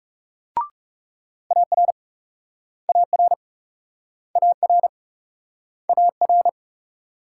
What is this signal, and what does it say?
Morse code, heard as a keyed tone, about 39 wpm, 700 Hz. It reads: AR AR AR AR